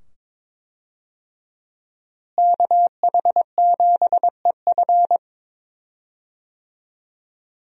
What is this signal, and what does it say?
Morse code, heard as a keyed tone, about 22 wpm, 700 Hz. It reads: KH7EF
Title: USA Call Signs - 5 Characters - Repeated 2x - 22wpm